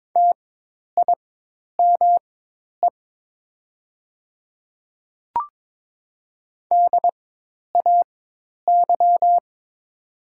Morse code, heard as a keyed tone, about 22 wpm, spaced as for 10 wpm, 700 Hz. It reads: TIME DAY
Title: 4x Character Spacing - Top 100 Words - 22wpm